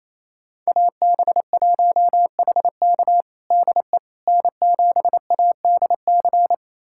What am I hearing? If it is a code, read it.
Morse code, heard as a keyed tone, about 28 wpm, 700 Hz. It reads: AB1HK DE N7ADC
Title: USA Callsign DE Callsign - 5 Characters - 28wpm